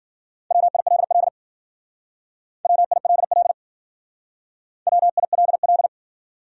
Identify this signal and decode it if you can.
Morse code, heard as a keyed tone, about 51 wpm, 700 Hz. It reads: WILL WILL WILL